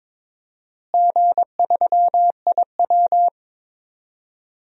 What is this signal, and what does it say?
Morse code, heard as a keyed tone, about 22 wpm, 700 Hz. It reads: G3IW
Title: DXCC-100 Call Signs - 4 Characters - Repeated 3x - 22wpm